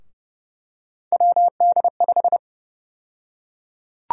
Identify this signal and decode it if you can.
Morse code, heard as a keyed tone, about 30 wpm, 700 Hz. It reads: WD5
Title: USA Call Sign Prefixes - Repeated 2x - 30wpm